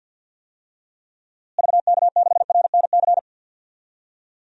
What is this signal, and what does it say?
Morse code, heard as a keyed tone, about 51 wpm, 700 Hz. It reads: VK6RNC
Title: DXCC-100 Call Signs - 6 Characters - Repeated 3x - 50wpm